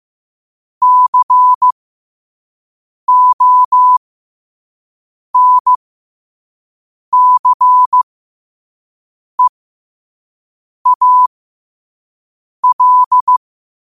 Morse code, heard as a keyed tone, about 15 wpm, spaced as for 5 wpm, 1000 Hz. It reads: CONCEAL
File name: qotdfiles / 5wpm